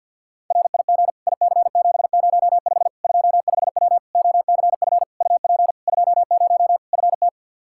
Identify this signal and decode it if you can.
Morse code, heard as a keyed tone, about 52 wpm, 700 Hz. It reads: RIG IC705 25W OCF UP 20 FT
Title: K1USN SST QSO Practice 50wpm